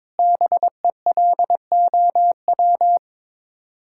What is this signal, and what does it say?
Morse code, heard as a keyed tone, about 22 wpm, 700 Hz. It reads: BELOW
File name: Sets of 9 Words - 22wpm